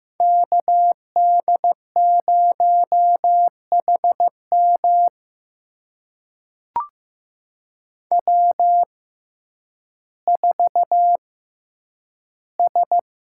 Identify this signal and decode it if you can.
Morse code, heard as a keyed tone, about 15 wpm, spaced as for 9 wpm, 700 Hz. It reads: KD0HM W4S
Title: USA Call Signs - 5 Characters - 6x Character Spacing - 15wpm